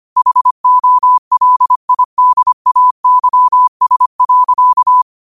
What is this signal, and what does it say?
Morse code, heard as a keyed tone, about 25 wpm, 1000 Hz. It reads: HOLIDAYS.
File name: qotdfiles / 25wpm